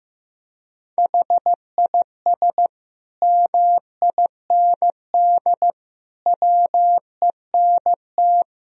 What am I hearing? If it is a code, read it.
Morse code, heard as a keyed tone, about 15 wpm, 700 Hz. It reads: HIS MIND WENT
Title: Sentences from Top 1000 Words - Review Part 3 - 15wpm